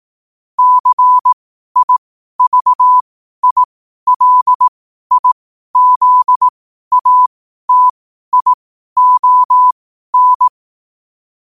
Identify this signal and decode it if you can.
Morse code, heard as a keyed tone, about 18 wpm, spaced as for 12 wpm, 1000 Hz. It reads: CIVILIZATION
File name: qotdfiles / 12wpm